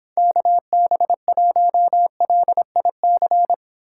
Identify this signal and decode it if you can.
Morse code, heard as a keyed tone, about 26 wpm, 700 Hz. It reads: KB1LIC